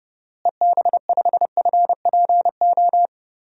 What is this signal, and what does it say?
Morse code, heard as a keyed tone, about 30 wpm, 700 Hz. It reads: EB5FPO
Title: DXCC-100 Call Signs - 6 Characters - Repeated 2x - 30wpm